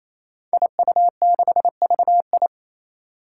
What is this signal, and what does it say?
Morse code, heard as a keyed tone, about 27 wpm, 700 Hz. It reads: IU6VI
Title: Five Challenging Letter-Number Combinations 28wpm